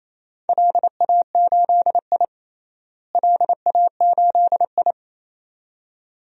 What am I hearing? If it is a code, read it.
Morse code, heard as a keyed tone, about 28 wpm, 700 Hz. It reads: LA8I LA8I
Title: DXCC-100 Call Signs - 4 Characters - Repeated 2x - 28wpm